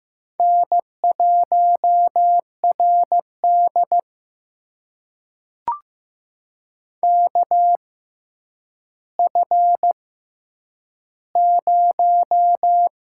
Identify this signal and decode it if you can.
Morse code, heard as a keyed tone, about 15 wpm, spaced as for 9 wpm, 700 Hz. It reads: N1RD KF0
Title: USA Call Signs - 4 Characters - 6x Character Spacing - 15wpm